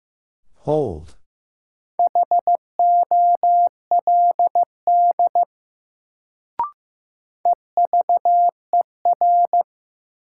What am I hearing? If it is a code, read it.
Morse code, heard as a keyed tone, about 15 wpm, 700 Hz. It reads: HOLD EVER
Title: Sentences from Top 400 Words - New Word Drill - 15wpm